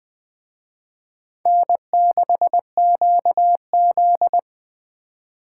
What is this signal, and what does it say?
Morse code, heard as a keyed tone, about 24 wpm, 700 Hz. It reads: N6QZ
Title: Speed Racing - USA 1x2 Call Signs 20wpm